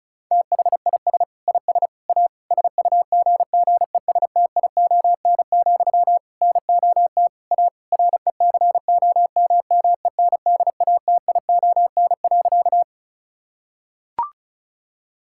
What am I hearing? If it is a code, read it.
Morse code, heard as a keyed tone, about 36 wpm, 700 Hz. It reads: THIS IS A SUGGESTION, NOT A RECOMMENDATION.
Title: Sentences from Top 1600 Words 35wpm